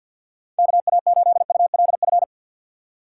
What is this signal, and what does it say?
Morse code, heard as a keyed tone, about 53 wpm, 700 Hz. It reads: KA8ULF
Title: USA Call Signs - 6x Character Spacing - 50wpm